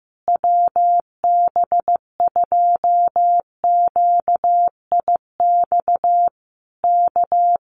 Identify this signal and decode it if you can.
Morse code, heard as a keyed tone, about 15 wpm, 700 Hz. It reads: WB2QIX K
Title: Common QSO Phrases Encore 15wpm